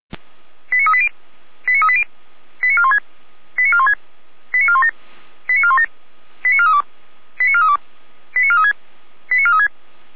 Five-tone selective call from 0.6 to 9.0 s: 89380 89380 89416 89416 89417 89417 89432 89432 89436

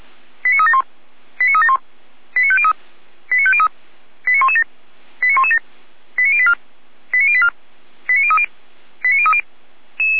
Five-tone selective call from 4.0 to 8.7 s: 89117 89117 89084 89084 89030

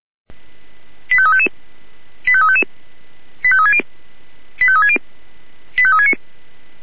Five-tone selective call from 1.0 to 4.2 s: 85355 85355 85360

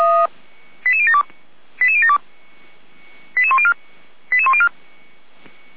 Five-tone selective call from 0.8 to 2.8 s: 88072 88072

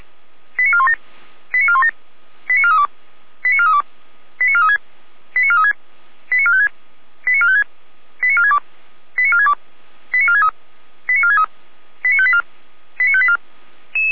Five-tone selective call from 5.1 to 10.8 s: 89436 89456 89456 89462 89462 89463

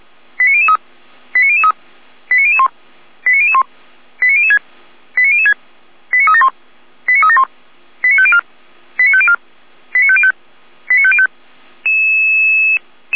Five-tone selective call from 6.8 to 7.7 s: 89371

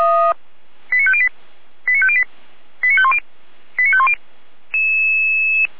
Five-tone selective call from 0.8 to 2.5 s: 89508 89508

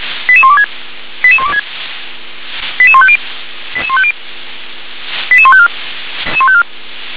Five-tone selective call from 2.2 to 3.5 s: 88150

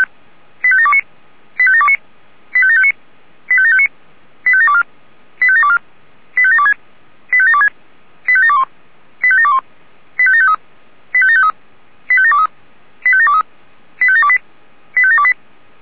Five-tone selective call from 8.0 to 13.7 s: 85721 85721 85753 85753 85723 85723

